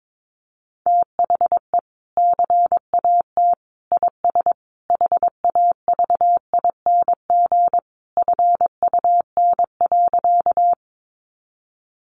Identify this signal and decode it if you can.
Morse code, heard as a keyed tone, about 22 wpm, 700 Hz. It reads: THE CAT IS HAVING FUN.
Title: Easy Phrases 22wpm